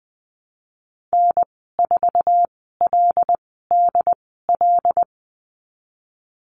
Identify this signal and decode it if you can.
Morse code, heard as a keyed tone, about 20 wpm, spaced as for 14 wpm, 700 Hz. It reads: N4LDL
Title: 122 Lesson 20g - Number 3 - Call Signs 20wpm